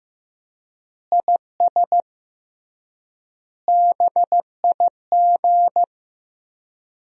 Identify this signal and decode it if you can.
Morse code, heard as a keyed tone, about 15 wpm, 700 Hz. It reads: IS BIG
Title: 3x Word-Spacing - Sentences from Top 400 Words - 15wpm